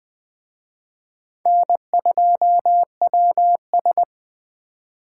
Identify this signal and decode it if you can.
Morse code, heard as a keyed tone, about 21 wpm, 700 Hz. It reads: N2WS